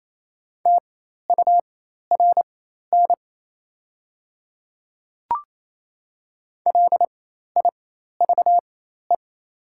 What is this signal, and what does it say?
Morse code, heard as a keyed tone, about 28 wpm, spaced as for 13 wpm, 690 Hz. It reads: TURN LIVE
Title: 4x Character Spacing - Top 200 Words - 28wpm